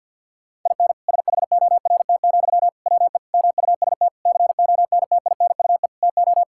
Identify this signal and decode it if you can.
Morse code, heard as a keyed tone, about 51 wpm, 700 Hz. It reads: IN SHORT, WE MUST CONTINUE TO
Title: Sentences from Top 1600 Words 50wpm